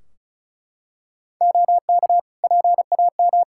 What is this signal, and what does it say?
Morse code, heard as a keyed tone, about 36 wpm, 700 Hz. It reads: OK PAM
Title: Practice QSO - Exchange 3 - More Info - 35wpm